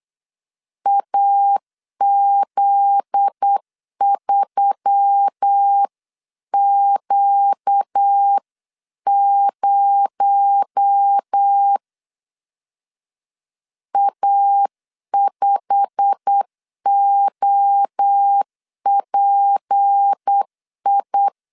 Morse code, heard as a keyed tone, about 8 wpm, 800 Hz. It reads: AZ3Q0 A5OPI
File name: LEZIONI / 8 YJ45 / 8 YJ45 8wpm 3214